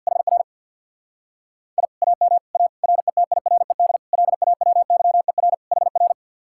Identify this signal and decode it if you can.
Morse code, heard as a keyed tone, about 64 wpm, 700 Hz. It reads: HR I AM A RETIRED LAWYER HR